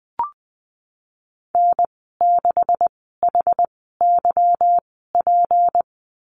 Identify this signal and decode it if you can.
Morse code, heard as a keyed tone, about 19 wpm, spaced as for 14 wpm, 700 Hz. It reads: N6HYP
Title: 161 Lesson 26g - Letter g - Call Signs 20wpm